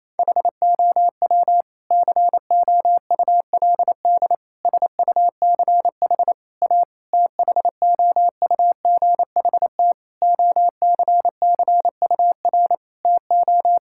Morse code, heard as a keyed tone, about 28 wpm, 700 Hz. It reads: HOW COULD SUCH A THOUGHT OCCUR TO
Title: Sentences from Top 1000 Words - Review Part 6 - 28wpm